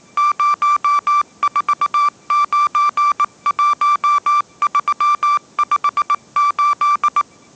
Morse code, heard as a keyed tone, about 19 wpm, 1170 Hz. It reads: 0491358